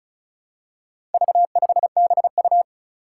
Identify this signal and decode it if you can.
Morse code, heard as a keyed tone, about 36 wpm, 700 Hz. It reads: V5BU
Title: Four Challenging Letter-Number Combinations 35wpm